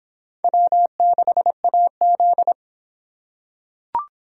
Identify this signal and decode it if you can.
Morse code, heard as a keyed tone, about 23 wpm, 700 Hz. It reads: W6AZ